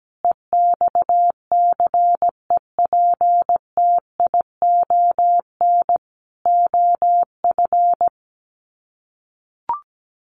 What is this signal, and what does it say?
Morse code, heard as a keyed tone, about 17 wpm, 700 Hz. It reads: EXCEPTION OF
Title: Sets of 4 Words 17wpm